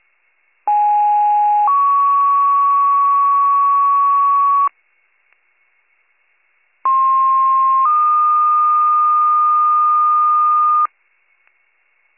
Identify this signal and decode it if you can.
Two-tone page, heard as a one-second tone, then a three-second tone, about 830 then 1120 Hz, about 1060 then 1220 Hz. It is heard from 0.7 to 10.9 s.